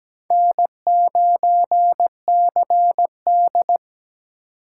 Morse code, heard as a keyed tone, about 20 wpm, 700 Hz. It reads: N9CD
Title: Speed Racing - USA 1x2 Call Signs 17wpm